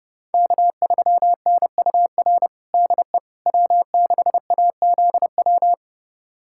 Morse code, heard as a keyed tone, about 32 wpm, 700 Hz. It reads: K3NUR DE W6AZW
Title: USA Callsign DE Callsign - 5 Characters - 30wpm